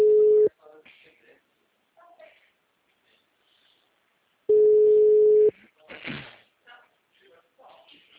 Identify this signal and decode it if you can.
Ringback tone (Europe, 425 Hz), with speech.